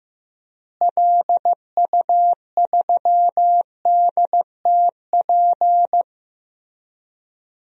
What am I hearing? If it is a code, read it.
Morse code, heard as a keyed tone, about 15 wpm, 700 Hz. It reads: LU3DTP
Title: DXCC-100 Call Signs - 6 Characters - Repeated 3x - 15wpm